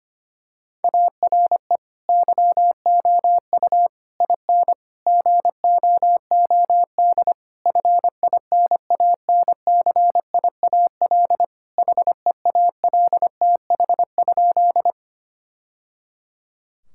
Morse code, heard as a keyed tone, about 25 wpm, 700 Hz. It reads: ARE YOU IN GOOD FINANCIAL HEALTH?
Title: Sentences from Top 1000 Words - Review Part 5 - 25wpm